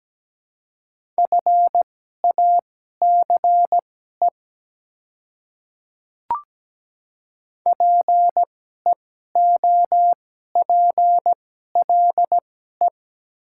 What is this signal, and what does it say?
Morse code, heard as a keyed tone, about 17 wpm, spaced as for 12 wpm, 700 Hz. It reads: FACE PEOPLE
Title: 2x Character Spacing - Top 300 Words - 17wpm